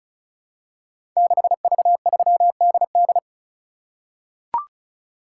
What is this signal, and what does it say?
Morse code, heard as a keyed tone, about 36 wpm, 700 Hz. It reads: 6V3DD
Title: Code Groups - 5 Letter:Numbers - 35wpm